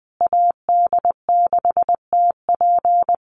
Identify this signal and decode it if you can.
Morse code, heard as a keyed tone, about 20 wpm, 700 Hz. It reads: AD6TP
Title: USA Call Signs Repeated 3x 20wpm